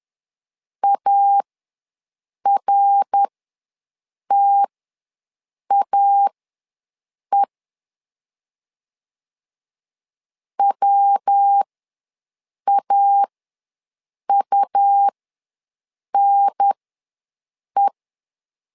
Morse code, heard as a keyed tone, about 10 wpm, spaced as for 6 wpm, 800 Hz. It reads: ARTAE WAUNE
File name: LEZIONI / 3 MWRO / 3 MWRO ws 10wpm 2310